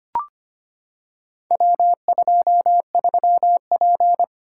Morse code, heard as a keyed tone, about 25 wpm, 700 Hz. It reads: W23P
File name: practice / 25wpm / Four Letter-Number Combinations - 25wpm